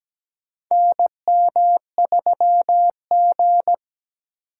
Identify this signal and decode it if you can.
Morse code, heard as a keyed tone, about 17 wpm, 700 Hz. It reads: NM3G
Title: USA Call Signs - 4 Characters - 2x Character Spacing - 17wpm